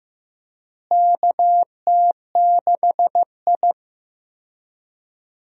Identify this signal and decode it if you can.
Morse code, heard as a keyed tone, about 15 wpm, 700 Hz. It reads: KT6I
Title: USA Call Signs - 4 Characters - Repeated 3x - 15wpm